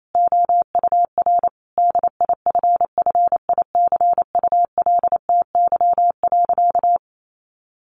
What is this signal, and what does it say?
Morse code, heard as a keyed tone, about 28 wpm, 700 Hz. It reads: OUR DIFFICULTY.
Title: Sentences from Top 1500 Words 28wpm